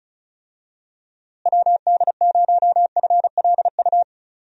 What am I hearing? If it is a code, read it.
Morse code, heard as a keyed tone, about 34 wpm, 700 Hz. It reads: WD0FLU
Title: USA Call Signs - 6 Characters - Repeated 3x - 35wpm